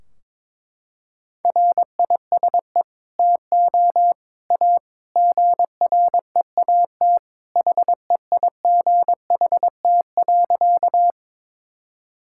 Morse code, heard as a keyed tone, about 22 wpm, 700 Hz. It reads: RISE TO A GREAT HEIGHT.
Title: Sentences from Top 1600 Words 22wpm